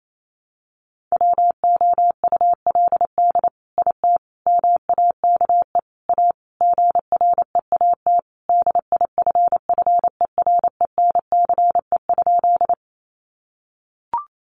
Morse code, heard as a keyed tone, about 28 wpm, 700 Hz. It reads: WOULD IT MAKE A GREAT DIFFERENCE?
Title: Sentences from Top 800 Words 28wpm